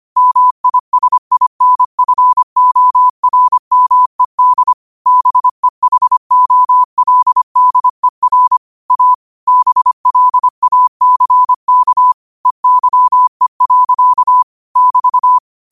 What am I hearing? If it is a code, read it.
Morse code, heard as a keyed tone, about 25 wpm, 1000 Hz. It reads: MISINFORMED BEHOLDER A BLACK EYE. <BT>
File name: qotdfiles / 25wpm